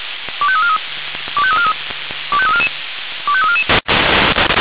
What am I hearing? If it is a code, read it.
Five-tone selective call (ZVEI), heard as quick runs of five tones, tones of about 70 ms. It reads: 25343 25343 25344 25344
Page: SDR/ZVEI